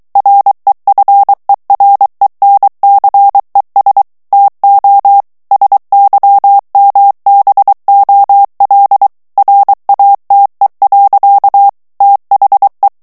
Morse code, heard as a keyed tone, about 23 wpm, 800 Hz. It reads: REFERENCES TO SYMBOL RATE. THE
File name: ANGLAIS texte clair 3 vitesse 22 mots minute